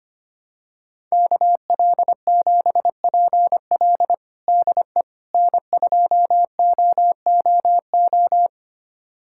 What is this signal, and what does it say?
Morse code, heard as a keyed tone, about 25 wpm, 700 Hz. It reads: KL7PL DE N2OOO
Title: USA Callsign DE Callsign - 5 Characters - 25wpm